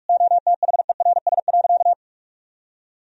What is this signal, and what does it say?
Morse code, heard as a keyed tone, about 46 wpm, 700 Hz. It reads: OTHERS.
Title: Sentences from Top 1000 Words - Review Part 2 - 45wpm